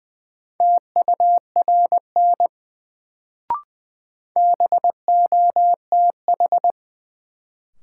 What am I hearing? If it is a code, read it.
Morse code, heard as a keyed tone, about 20 wpm, 700 Hz. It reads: TURN BOTH
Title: Sets of 4 Words - New Word Drill - 20wpm